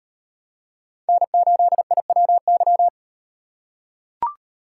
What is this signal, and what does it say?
Morse code, heard as a keyed tone, about 36 wpm, 700 Hz. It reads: N8IWY